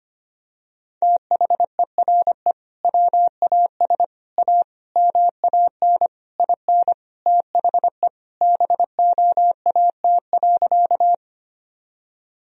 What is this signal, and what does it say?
Morse code, heard as a keyed tone, about 25 wpm, 700 Hz. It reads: THERE WAS A MAN IN THE BOAT.
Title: Sentences from Top 1000 Words - Review Part 3 - 25wpm